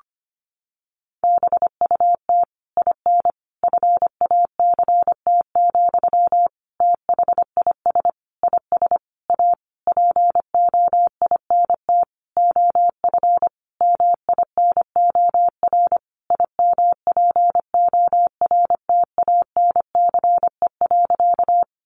Morse code, heard as a keyed tone, about 25 wpm, 700 Hz. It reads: BUT IN FACT, THIS IS A POINT OF MINOR IMPORTANCE.
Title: Sentences from Top 1500 Words 25wpm